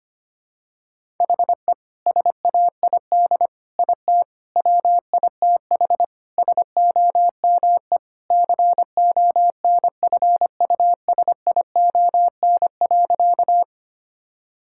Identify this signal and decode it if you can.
Morse code, heard as a keyed tone, about 25 wpm, 700 Hz. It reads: HE SAID IT WITH SOME CONFUSION.
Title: Sentences from Top 1600 Words 25wpm